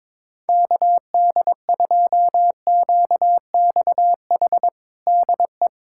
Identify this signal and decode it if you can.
Morse code, heard as a keyed tone, about 22 wpm, 700 Hz. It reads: KD2QXH DE
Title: USA Callsign DE Callsign - 6 Characters - 22wpm